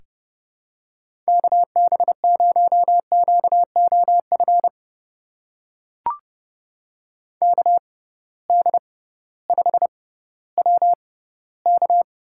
Morse code, heard as a keyed tone, about 29 wpm, spaced as for 16 wpm, 700 Hz. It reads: KB0QOF KD5WK
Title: USA Call Signs - 6 Characters - 6x Character Spacing - 30wpm